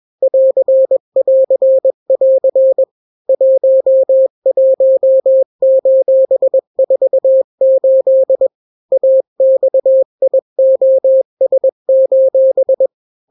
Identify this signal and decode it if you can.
Morse code, heard as a keyed tone, about 21 wpm, 530 Hz. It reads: <AR><AR><AR> 11:48 AXIOS:
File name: daily / 20wpm